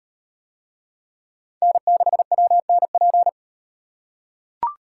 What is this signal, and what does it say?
Morse code, heard as a keyed tone, about 35 wpm, 700 Hz. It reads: N6WNP